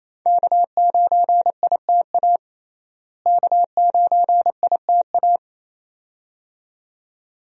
Morse code, heard as a keyed tone, about 28 wpm, 700 Hz. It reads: K9ITA K9ITA
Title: USA Call Signs - 5 Characters - Repeated 2x - 28wpm